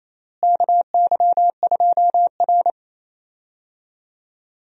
Morse code, heard as a keyed tone, about 28 wpm, 700 Hz. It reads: KY2R
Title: USA Call Signs - 4 Characters - Repeated 3x - 28wpm